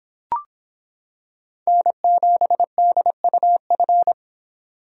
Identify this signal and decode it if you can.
Morse code, heard as a keyed tone, about 26 wpm, 700 Hz. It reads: N7DUF